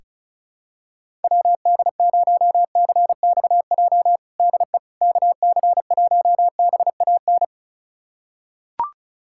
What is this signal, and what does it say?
Morse code, heard as a keyed tone, about 36 wpm, 700 Hz. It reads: WD0CXJ DE KC1BAN
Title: USA Callsign DE Callsign - 6 Characters - 35wpm